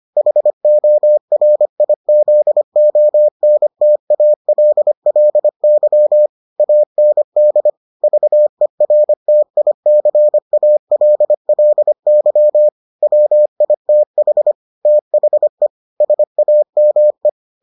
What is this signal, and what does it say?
Morse code, heard as a keyed tone, about 25 wpm, 600 Hz. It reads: HORIZONTALLY AND VERTICALLY WITH THE SAME